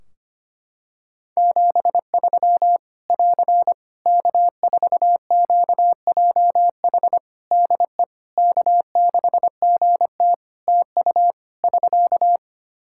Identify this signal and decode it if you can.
Morse code, heard as a keyed tone, about 25 wpm, 700 Hz. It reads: 73 <AR> K4QJH DE K6GT TU <SK>
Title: Practice QSO - Exchange 4 - Ending - 25wpm